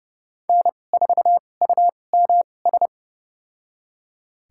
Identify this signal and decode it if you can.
Morse code, heard as a keyed tone, about 30 wpm, spaced as for 22 wpm, 700 Hz. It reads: N4UMS
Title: USA Call Signs - 5 Characters - 2x Character Spacing - 30wpm